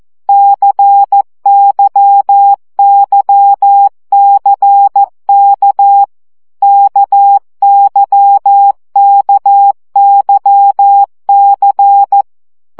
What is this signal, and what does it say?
Morse code, heard as a keyed tone, about 14 wpm, 800 Hz. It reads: CYYCK KYKYC
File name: CW gyakorlas k c y 12 baud